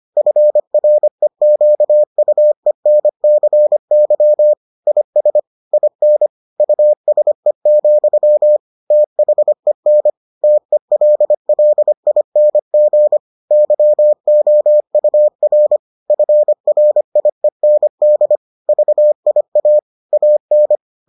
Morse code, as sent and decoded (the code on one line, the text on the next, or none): ..-. .-. --.- ..- . -. -.-. -.-- / .. ... / .. -. / ..- ... --..-- / - .... -. / - . .-.. .-.. .. -. --. / -.-- --- ..- .-. / ..-. .-. .. -. -.. / ...- .. .- / .- -.
FREQUENCY IS IN USE, THEN TELLING YOUR FRIEND VIA AN